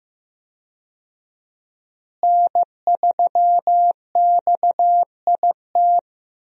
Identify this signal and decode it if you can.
Morse code, heard as a keyed tone, about 15 wpm, 700 Hz. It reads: N3XIT
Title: USA Call Signs - 5 Characters - Repeated 3x - 15wpm